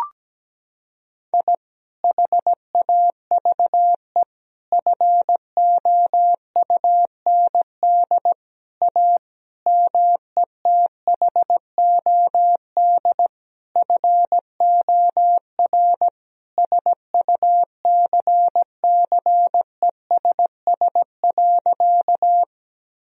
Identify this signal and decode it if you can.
Morse code, heard as a keyed tone, about 17 wpm, 700 Hz. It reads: I HAVE FOUND A METHOD FOR SUCCESS.
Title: Sentences from Top 1000 Words - Review Part 3 - 17wpm